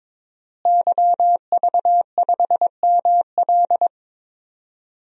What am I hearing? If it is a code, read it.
Morse code, heard as a keyed tone, about 22 wpm, 700 Hz. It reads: YV5ML
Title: DXCC-100 Call Signs - 5 Characters - Repeated 3x - 22wpm